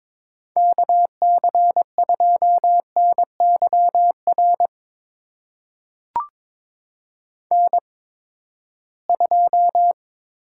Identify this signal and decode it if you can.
Morse code, heard as a keyed tone, about 23 wpm, spaced as for 15 wpm, 700 Hz. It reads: KC2NYR N2
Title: USA Call Signs - 8x Character Spacing - 22wpm